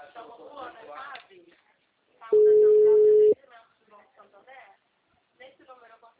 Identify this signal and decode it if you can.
Ringback tone (Europe, 425 Hz), with speech.
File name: out- -2033- - - .72157